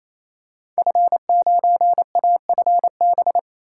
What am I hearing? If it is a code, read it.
Morse code, heard as a keyed tone, about 28 wpm, 700 Hz. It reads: F9AFB
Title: Code Groups - 5 Letter:Numbers - 28wpm